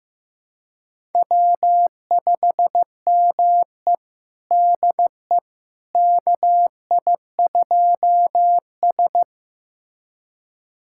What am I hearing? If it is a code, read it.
Morse code, heard as a keyed tone, about 15 wpm, 700 Hz. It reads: W5ME DE KI2S
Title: USA Callsign DE Callsign - 4 Characters - 15wpm